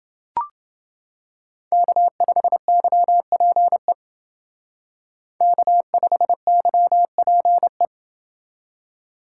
Morse code, heard as a keyed tone, about 28 wpm, 700 Hz. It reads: K5YPE K5YPE